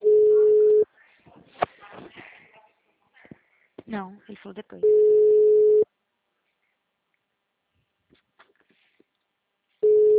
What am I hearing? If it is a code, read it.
Ringback tone (Europe, 425 Hz), with speech.